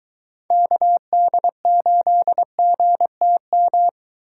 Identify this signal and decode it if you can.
Morse code, heard as a keyed tone, about 22 wpm, 700 Hz. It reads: KD8GTM